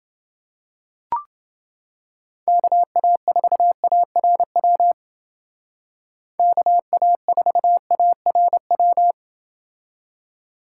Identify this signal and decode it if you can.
Morse code, heard as a keyed tone, about 28 wpm, 700 Hz. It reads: KA4ARW KA4ARW